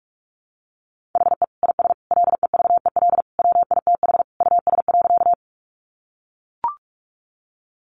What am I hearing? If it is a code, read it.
Morse code, heard as a keyed tone, about 44 wpm, 700 Hz. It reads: HE IS LEVEL WITH US.
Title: Sentences from Top 700 Words 45wpm